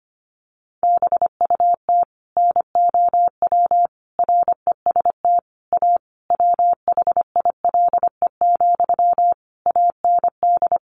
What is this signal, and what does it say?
Morse code, heard as a keyed tone, about 25 wpm, 700 Hz. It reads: BUT NOW REST A WHILE, AND
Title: Sentences from Top 1100 Words 25wpm